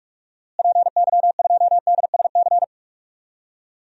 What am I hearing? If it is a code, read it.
Morse code, heard as a keyed tone, about 46 wpm, 700 Hz. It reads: PY2DSC
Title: DXCC-100 Call Signs - 6 Characters - Repeated 2x - 45wpm